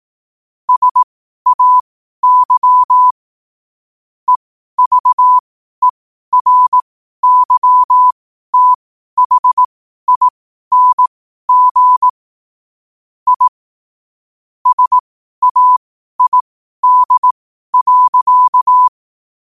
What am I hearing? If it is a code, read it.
Morse code, heard as a keyed tone, about 18 wpm, spaced as for 12 wpm, 1000 Hz. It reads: SAY EVERYTHING I SAID.
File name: qotdfiles / 12wpm